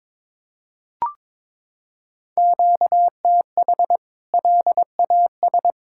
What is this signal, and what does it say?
Morse code, heard as a keyed tone, about 22 wpm, 700 Hz. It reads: QTH LAS
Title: K1USN SST QSO Practice 22wpm